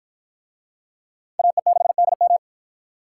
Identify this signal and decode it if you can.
Morse code, heard as a keyed tone, about 53 wpm, 700 Hz. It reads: AE6DM